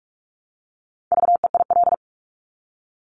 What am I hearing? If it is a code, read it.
Morse code, heard as a keyed tone, about 46 wpm, 700 Hz. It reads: VEIL
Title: Multi-letter Phonemes Repeated 3x 45wpm